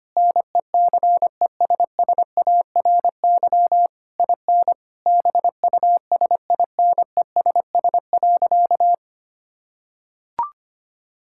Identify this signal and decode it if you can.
Morse code, heard as a keyed tone, about 25 wpm, 700 Hz. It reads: NECESSARY IN BUSINESS.
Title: Sentences from Top 1000 Words - Review Part 6 - 25wpm